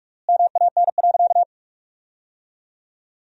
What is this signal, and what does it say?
Morse code, heard as a keyed tone, about 46 wpm, 700 Hz. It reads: MAN.
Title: 4x Word-Spacing - Sentences from Top 400 Words - 45wpm